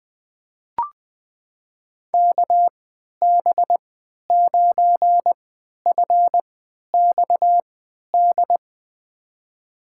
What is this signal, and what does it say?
Morse code, heard as a keyed tone, about 20 wpm, spaced as for 11 wpm, 700 Hz. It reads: KB9FXD
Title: USA Call Signs - 6 Characters - 3x Character Spacing - 20wpm